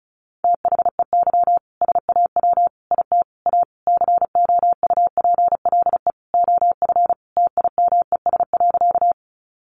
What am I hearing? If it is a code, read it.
Morse code, heard as a keyed tone, about 34 wpm, 700 Hz. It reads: THEY SAW IT A COUPLE OF TIMES.
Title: Sentences from Top 1100 Words 35wpm